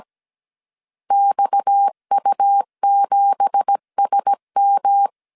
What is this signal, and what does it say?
Morse code, heard as a keyed tone, about 17 wpm, 800 Hz. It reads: XU7SM
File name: CW / DET / 1 9 1500C S C 16wpm 2909